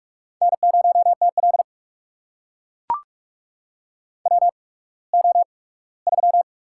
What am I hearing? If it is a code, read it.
Morse code, heard as a keyed tone, about 44 wpm, spaced as for 21 wpm, 690 Hz. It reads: N0TL WO3
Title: USA Call Signs - 4 Characters - 8x Character Spacing - 45wpm